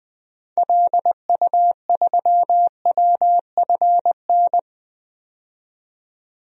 Morse code, heard as a keyed tone, about 21 wpm, 700 Hz. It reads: LU3WFN
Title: DXCC-100 Call Signs - 6 Characters - 20wpm